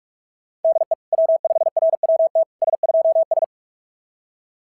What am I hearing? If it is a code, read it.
Morse code, heard as a keyed tone, about 38 wpm, 660 Hz. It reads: DE W5RWT S2S
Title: SOTA QSO Practice 45wpm